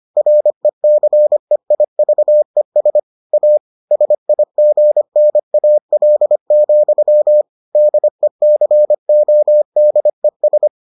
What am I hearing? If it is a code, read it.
Morse code, heard as a keyed tone, about 25 wpm, 600 Hz. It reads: RECEIVES A SIGNAL, DECODES